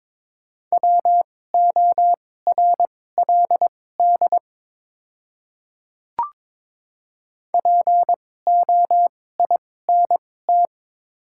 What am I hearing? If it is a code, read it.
Morse code, heard as a keyed tone, about 21 wpm, spaced as for 16 wpm, 700 Hz. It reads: WORLD POINT
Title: 2x Character Spacing - Top 300 Words - 22wpm